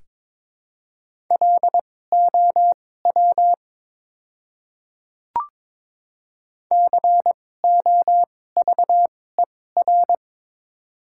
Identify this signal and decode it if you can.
Morse code, heard as a keyed tone, about 23 wpm, spaced as for 16 wpm, 700 Hz. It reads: LOW COVER
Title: 2x Character Spacing - Top 300 Words - 22wpm